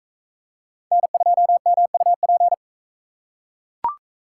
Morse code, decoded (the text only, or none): N2MUP